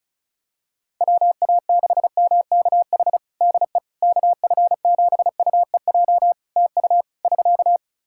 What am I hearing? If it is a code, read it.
Morse code, heard as a keyed tone, about 34 wpm, 700 Hz. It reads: WA6MKH DE KF7UEJ TU <SK>
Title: K1USN SST QSO Practice 35wpm